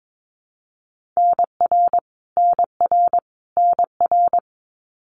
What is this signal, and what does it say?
Morse code, heard as a keyed tone, about 22 wpm, 700 Hz. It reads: NR NR NR
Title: Abbreviations Repeated 3x - 22wpm